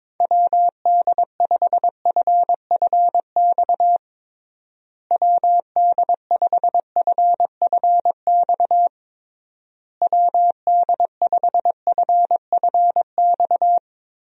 Morse code, heard as a keyed tone, about 22 wpm, 700 Hz. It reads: WD5FFX WD5FFX WD5FFX
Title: USA Call Signs - 6 Characters - Repeated 3x - 22wpm